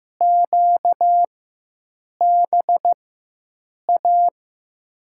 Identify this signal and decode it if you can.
Morse code, heard as a keyed tone, about 15 wpm, spaced as for 7 wpm, 700 Hz. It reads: QBA
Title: USA Call Signs - 6 Characters - 4x Character Spacing - 15wpm